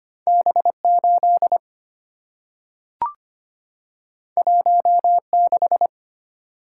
Morse code, heard as a keyed tone, about 25 wpm, 700 Hz. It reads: B 8 1 6
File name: Two Letter-Number Combinations - 25wpm